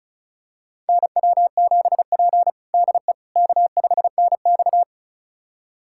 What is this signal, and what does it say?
Morse code, heard as a keyed tone, about 36 wpm, 700 Hz. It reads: NW7P DE K5NX
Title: USA Callsign DE Callsign - 4 Characters - 35wpm